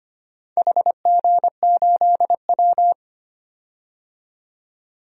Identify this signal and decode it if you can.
Morse code, heard as a keyed tone, about 25 wpm, 700 Hz. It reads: HG8W
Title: DXCC-100 Call Signs - 4 Characters - 25wpm